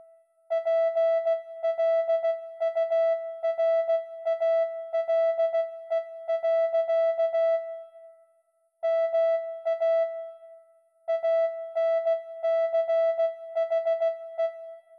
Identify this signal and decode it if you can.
Morse code, heard as a keyed tone, about 16 wpm, 660 Hz. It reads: PLURALE. MA ANCHE